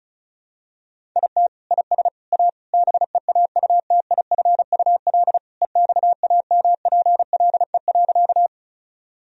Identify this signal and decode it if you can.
Morse code, heard as a keyed tone, about 34 wpm, 700 Hz. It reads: IT IS A BEAUTIFUL EXAMPLE.
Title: Sentences from Top 1000 Words - Review Part 1 - 35wpm